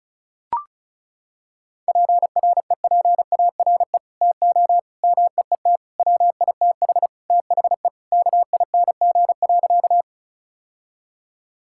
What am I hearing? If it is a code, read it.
Morse code, heard as a keyed tone, about 36 wpm, 700 Hz. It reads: PREPARE TO MEET WITH THE KING.
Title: Sentences from Top 1000 Words - Review Part 6 - 35wpm